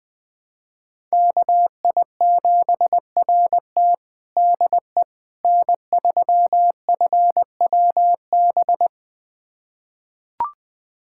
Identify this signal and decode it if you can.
Morse code, heard as a keyed tone, about 20 wpm, 700 Hz. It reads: KI7RT DE N3FWB
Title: USA Callsign DE Callsign - 5 Characters - 20wpm